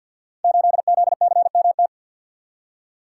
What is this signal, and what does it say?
Morse code, heard as a keyed tone, about 47 wpm, 700 Hz. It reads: 8ZCMT
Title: Code Groups - 5 Letter:Numbers - 50wpm